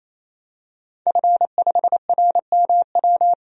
Morse code, heard as a keyed tone, about 28 wpm, 700 Hz. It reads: F5RMW
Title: DXCC-100 Call Signs - 5 Characters - 28wpm